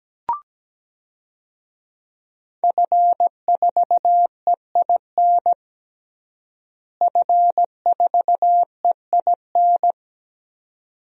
Morse code, heard as a keyed tone, about 17 wpm, 700 Hz. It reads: F4EIN F4EIN
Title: DXCC-100 Call Signs - 5 Characters - Repeated 2x - 17wpm